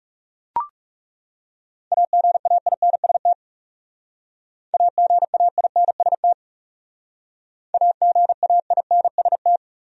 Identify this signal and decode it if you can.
Morse code, heard as a keyed tone, about 37 wpm, 700 Hz. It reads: AGAINST AGAINST AGAINST
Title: Speed-Racing Top 500 Words 30wpm